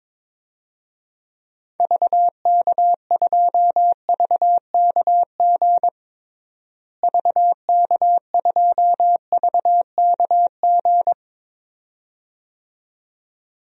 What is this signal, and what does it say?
Morse code, heard as a keyed tone, about 21 wpm, 700 Hz. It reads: VK2VKG VK2VKG
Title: DXCC-100 Call Signs - 6 Characters - Repeated 2x - 22wpm